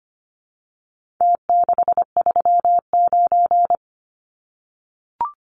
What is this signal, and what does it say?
Morse code, heard as a keyed tone, about 25 wpm, 700 Hz. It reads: T639
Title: Numbers - Contest Style - 4-Digit 25wpm